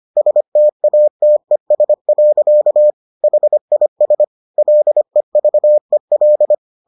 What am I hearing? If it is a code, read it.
Morse code, heard as a keyed tone, about 25 wpm, 600 Hz. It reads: STATES. HIS LEVEL